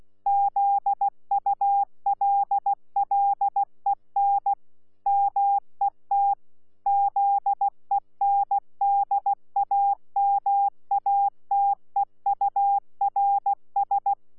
Morse code, heard as a keyed tone, about 16 wpm, 800 Hz. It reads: ZULLEN MET ZENDAMATEURS
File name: Z PI4VRZ / PI4VRZ 14-jun-2025